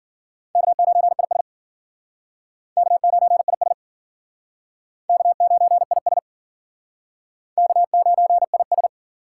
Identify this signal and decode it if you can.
Morse code, heard as a keyed tone, about 52 wpm, 700 Hz. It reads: K9IS K9IS K9IS K9IS